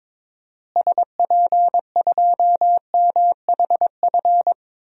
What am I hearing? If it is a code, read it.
Morse code, heard as a keyed tone, about 22 wpm, 700 Hz. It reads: SP2MHF
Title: DXCC-100 Call Signs - 6 Characters - Repeated 2x - 22wpm